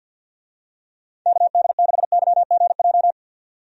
Morse code, heard as a keyed tone, about 46 wpm, 700 Hz. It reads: KD6YGJ
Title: USA Call Signs - 6 Characters - 50wpm